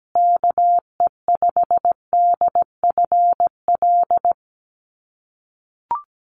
Morse code, heard as a keyed tone, about 17 wpm, 700 Hz. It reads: KE5DFL
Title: USA Call Signs Repeated 3x 17wpm